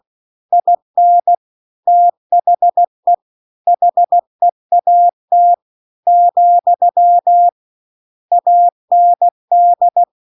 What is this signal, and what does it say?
Morse code, heard as a keyed tone, about 16 wpm, 700 Hz. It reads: IN THE HEAT , AND